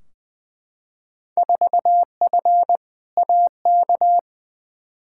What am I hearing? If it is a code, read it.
Morse code, heard as a keyed tone, about 20 wpm, 700 Hz. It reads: 4F AK
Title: Field Day Exchanges Encore - 20wpm